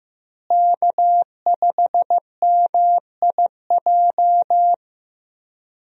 Morse code, heard as a keyed tone, about 15 wpm, 700 Hz. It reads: K5MIJ
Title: USA Call Signs - 5 Characters - Repeated 3x - 15wpm